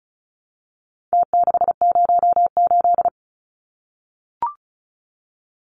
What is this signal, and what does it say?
Morse code, heard as a keyed tone, about 34 wpm, 700 Hz. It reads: T608
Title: Numbers - Contest Style - 4-Digit 35wpm